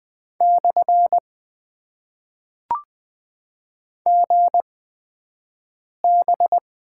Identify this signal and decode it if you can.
Morse code, heard as a keyed tone, about 20 wpm, spaced as for 5 wpm, 700 Hz. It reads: / GB
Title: Top 90 QSO Elements - 8x Character Spacing - 20wpm